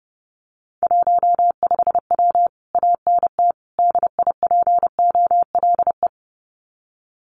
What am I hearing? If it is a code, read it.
Morse code, heard as a keyed tone, about 28 wpm, 700 Hz. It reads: 15W ANT DIPOLE
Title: Common QSO Phrases Encore 30wpm